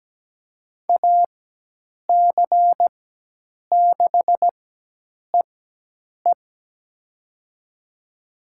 Morse code, heard as a keyed tone, about 17 wpm, spaced as for 8 wpm, 700 Hz. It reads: AC6EE
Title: USA Call Signs - 5 Characters - 4x Character Spacing - 17wpm